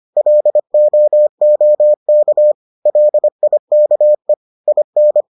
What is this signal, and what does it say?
Morse code, heard as a keyed tone, about 25 wpm, 600 Hz. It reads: LOOK LIKE IN